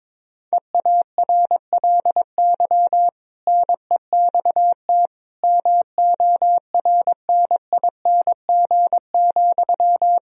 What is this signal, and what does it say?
Morse code, heard as a keyed tone, about 22 wpm, 700 Hz. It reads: EARLY NEXT MORNING,
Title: Sentences from Top 1000 Words - Review Part 4 - 22wpm